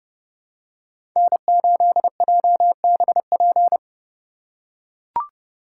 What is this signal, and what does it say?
Morse code, heard as a keyed tone, about 28 wpm, 700 Hz. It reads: N8JBP